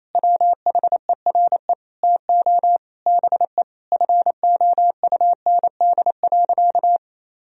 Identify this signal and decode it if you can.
Morse code, heard as a keyed tone, about 27 wpm, 700 Hz. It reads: WHERE TO BE FOUND.
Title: Sentences from Top 1000 Words - Review Part 3 - 28wpm